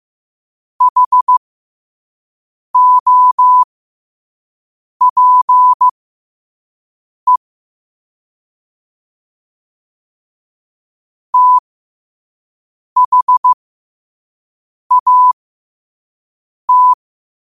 Morse code, heard as a keyed tone, about 15 wpm, spaced as for 5 wpm, 1000 Hz. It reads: HOPE THAT
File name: qotdfiles / 5wpm